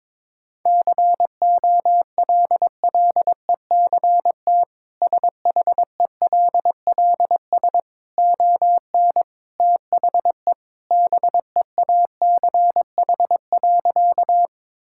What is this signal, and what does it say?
Morse code, heard as a keyed tone, about 22 wpm, 700 Hz. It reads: COLLECT SHELLS ON THE BEACH.
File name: practice / 22wpm / Easy Sentences with Hints - 22wpm